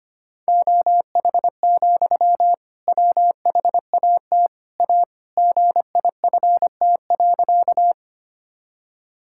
Morse code, heard as a keyed tone, about 25 wpm, 700 Hz. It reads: OH, WHAT A GIFT.
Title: Sentences from Top 1000 Words - Review Part 6 - 25wpm